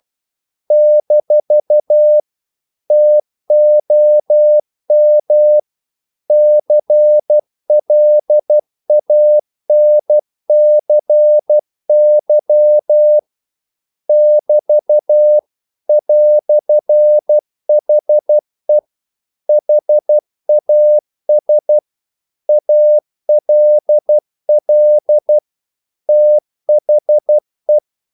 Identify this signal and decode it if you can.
Morse code, heard as a keyed tone, about 12 wpm, 600 Hz. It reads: - TOM CLANCY <BT> "HE HAS ALL THE
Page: English quotations in CW (telegraphy - morse)